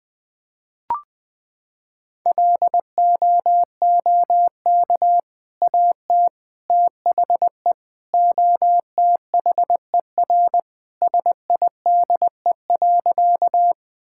Morse code, heard as a keyed tone, about 20 wpm, 700 Hz. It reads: LOOK AT THE OTHER SIDE.
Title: Sentences with 2-Word Phrases - 20wpm